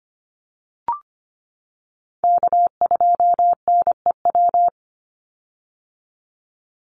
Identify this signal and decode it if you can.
Morse code, heard as a keyed tone, about 25 wpm, 700 Hz. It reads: K2NEW
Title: USA Call Signs 25wpm